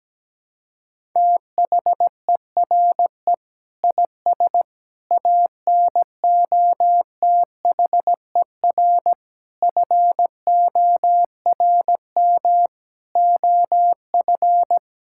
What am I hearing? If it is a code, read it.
Morse code, heard as a keyed tone, about 17 wpm, 700 Hz. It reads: THERE IS ANOTHER FORM OF
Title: Sentences from Top 1000 Words - Review Part 6 - 17wpm